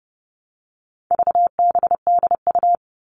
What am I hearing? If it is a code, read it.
Morse code, heard as a keyed tone, about 30 wpm, 700 Hz. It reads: VBDU
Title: All Cut Numbers - 4-Digit 30wpm